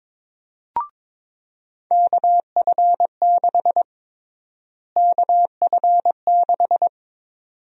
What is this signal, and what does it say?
Morse code, heard as a keyed tone, about 22 wpm, 700 Hz. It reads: KF6 KF6
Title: USA Call Sign Prefixes - Repeated 2x - 22wpm